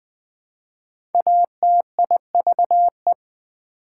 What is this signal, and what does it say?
Morse code, heard as a keyed tone, about 20 wpm, 700 Hz. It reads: ATIVE
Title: Prefixes and Suffixes 20wpm